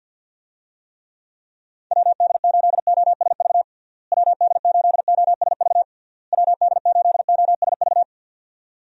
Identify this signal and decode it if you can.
Morse code, heard as a keyed tone, about 51 wpm, 700 Hz. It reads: WD8OSV WD8OSV WD8OSV
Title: USA Call Signs - 6 Characters - Repeated 3x - 50wpm